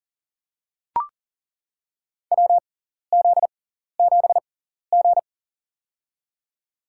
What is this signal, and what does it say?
Morse code, heard as a keyed tone, about 40 wpm, spaced as for 14 wpm, 700 Hz. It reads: WZ7G
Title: USA Call Signs - 4 Characters - 6x Character Spacing - 40wpm